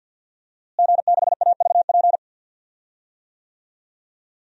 Morse code, heard as a keyed tone, about 47 wpm, 700 Hz. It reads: G6AVP
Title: DXCC-100 Call Signs - 5 Characters - Repeated 3x - 50wpm